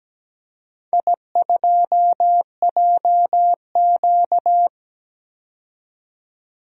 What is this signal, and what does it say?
Morse code, heard as a keyed tone, about 17 wpm, 700 Hz. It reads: I2JQ
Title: DXCC-100 Call Signs - 4 Characters - 17wpm